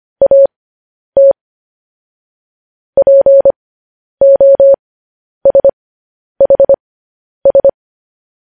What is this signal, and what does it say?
Morse code, heard as a keyed tone, about 25 wpm, spaced as for 10 wpm, 550 Hz. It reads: AT POSHS
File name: podcast / News headlines-1x-25x10